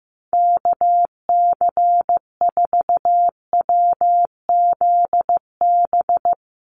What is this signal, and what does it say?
Morse code, heard as a keyed tone, about 15 wpm, 700 Hz. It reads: KC4WZB
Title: NA SOTA Activator Callsigns 15wpm